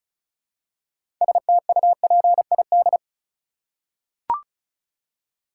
Morse code, heard as a keyed tone, about 34 wpm, 700 Hz. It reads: STUPID
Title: Six Letter Words - 35wpm